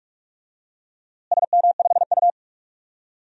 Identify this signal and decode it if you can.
Morse code, heard as a keyed tone, about 41 wpm, 700 Hz. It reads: SM5U
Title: DXCC-100 Call Signs - 4 Characters - Repeated 3x - 45wpm